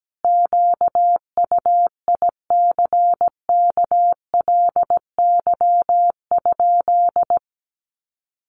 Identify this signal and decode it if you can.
Morse code, heard as a keyed tone, about 17 wpm, 700 Hz. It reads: QUICKLY?
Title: Sentences from Top 700 Words 17wpm